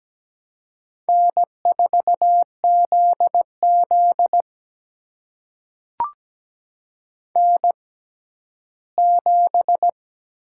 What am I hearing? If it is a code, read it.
Morse code, heard as a keyed tone, about 17 wpm, spaced as for 12 wpm, 700 Hz. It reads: N4ZZ N7
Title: USA Call Signs - 4 Characters - 6x Character Spacing - 17wpm